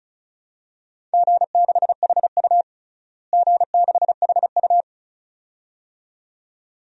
Morse code, heard as a keyed tone, about 36 wpm, 700 Hz. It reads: G6HU G6HU
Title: DXCC-100 Call Signs - 4 Characters - Repeated 2x - 35wpm